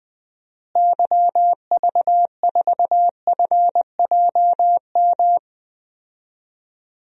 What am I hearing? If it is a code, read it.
Morse code, heard as a keyed tone, about 21 wpm, 700 Hz. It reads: YV4FJM
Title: DXCC-100 Call Signs - 6 Characters - Repeated 2x - 20wpm